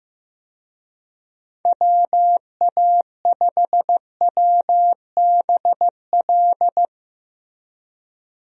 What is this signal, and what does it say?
Morse code, heard as a keyed tone, about 15 wpm, 700 Hz. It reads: WA5WBL
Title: USA Call Signs - 6 Characters - Repeated 3x - 15wpm